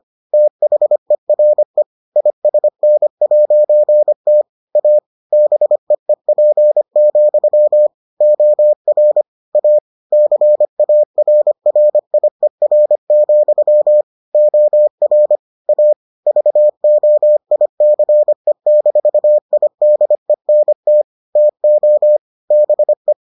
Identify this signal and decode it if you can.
Morse code, heard as a keyed tone, about 25 wpm, 600 Hz. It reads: THERE ISN'T A BEEP, OR A CARRIER, OR A VOICE-IDENT TO BE